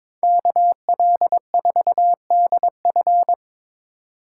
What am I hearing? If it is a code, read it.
Morse code, heard as a keyed tone, about 26 wpm, 700 Hz. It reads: KL4DF
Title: Speed Racing - USA Call Signs 22wpm